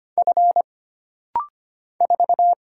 Morse code, heard as a keyed tone, about 25 wpm, 700 Hz. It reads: F 4
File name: Single Character (Letter:Number:Punctuation:Prosign) - Rapid-Fire - 25wpm